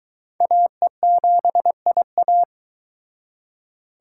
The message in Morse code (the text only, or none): AE7IA